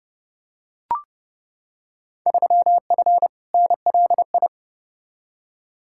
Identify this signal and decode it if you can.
Morse code, heard as a keyed tone, about 29 wpm, 700 Hz. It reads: 3F NLI
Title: Field Day Exchanges - 30wpm